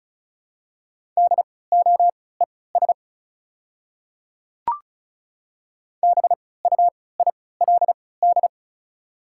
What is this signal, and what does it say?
Morse code, heard as a keyed tone, about 34 wpm, spaced as for 20 wpm, 700 Hz. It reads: DOES BUILD
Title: 3x Character Spacing - Top 300 Words - 35wpm